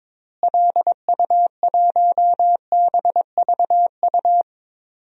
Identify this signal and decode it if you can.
Morse code, heard as a keyed tone, about 22 wpm, 700 Hz. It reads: LU1BVU
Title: DXCC-100 Call Signs - 6 Characters - Repeated 3x - 22wpm